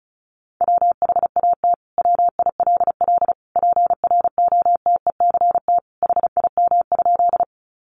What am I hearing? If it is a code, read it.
Morse code, heard as a keyed tone, about 34 wpm, 700 Hz. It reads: WHAT WILL PROTECT HIM?
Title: Sentences from Top 900 Words 35wpm